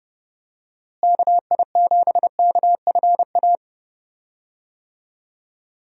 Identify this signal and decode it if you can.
Morse code, heard as a keyed tone, about 30 wpm, 700 Hz. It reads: KI7KFA
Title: USA Call Signs - 6 Characters - Repeated 3x - 30wpm